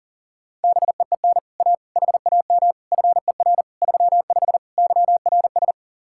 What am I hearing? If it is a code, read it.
Morse code, heard as a keyed tone, about 52 wpm, 700 Hz. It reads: BEEN A HAM FER 35 YRS